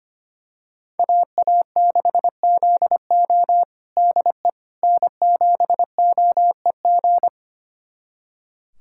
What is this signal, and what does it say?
Morse code, heard as a keyed tone, about 25 wpm, 700 Hz. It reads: AA6ZO DE N7OEG
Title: USA Callsign DE Callsign - 5 Characters - 25wpm